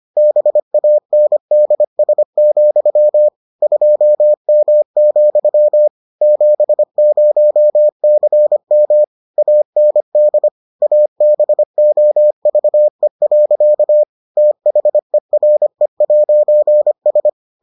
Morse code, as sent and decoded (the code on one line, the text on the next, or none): -... .- -. -.. ... --..-- / ..--- -- --..-- / --... ----- -.-. -- / .- -. -.. / .- -... --- ...- . .-.-.- / - .... .-. .----. ...
BANDS, 2M, 70CM AND ABOVE. THERE'S